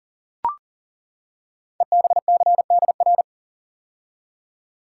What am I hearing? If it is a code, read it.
Morse code, heard as a keyed tone, about 39 wpm, 690 Hz. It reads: EBCDR
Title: Code Groups - 5 Letters - 40wpm